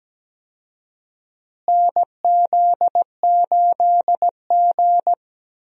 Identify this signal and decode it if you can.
Morse code, heard as a keyed tone, about 17 wpm, 700 Hz. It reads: NZ8G
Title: USA Call Signs - 4 Characters - Repeated 2x - 17wpm